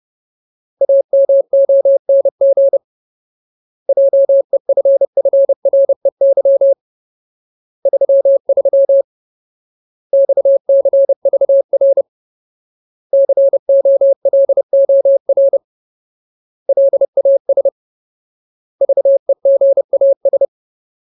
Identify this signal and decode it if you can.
Morse code, heard as a keyed tone, about 30 wpm, 550 Hz. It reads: AMONG JEFFREY 33 XCVR COLOR LAS VEGAS
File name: Ultimate Word List / 30WPM Ultimate